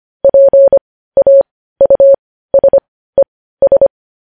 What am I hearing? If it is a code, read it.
Morse code, heard as a keyed tone, about 25 wpm, spaced as for 15 wpm, 550 Hz. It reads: PAUSES
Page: news-headlines-in-morse-code-at-15-wpm